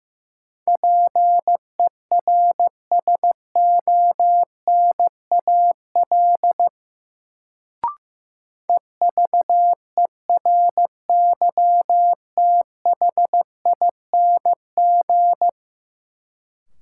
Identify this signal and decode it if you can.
Morse code, heard as a keyed tone, about 16 wpm, 700 Hz. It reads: PERSONAL EVERYTHING
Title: Sentences from Top 700 Words - New Word Drill - 15wpm